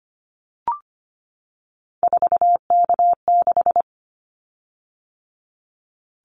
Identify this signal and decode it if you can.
Morse code, heard as a keyed tone, about 25 wpm, 700 Hz. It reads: 4K6
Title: Callsign Prefixes for Top 100 DXCC - 25wpm